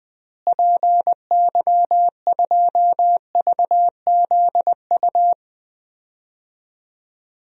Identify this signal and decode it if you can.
Morse code, heard as a keyed tone, about 21 wpm, 700 Hz. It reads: PY2VZU
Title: DXCC-100 Call Signs - 6 Characters - Repeated 3x - 20wpm